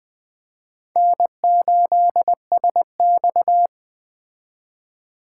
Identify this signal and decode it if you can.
Morse code, heard as a keyed tone, about 21 wpm, 700 Hz. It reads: N8SX